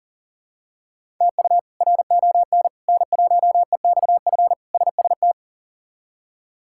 Morse code, heard as a keyed tone, about 40 wpm, 700 Hz. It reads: TU RON N1EXF SST
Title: K1USN SST QSO Practice - 40wpm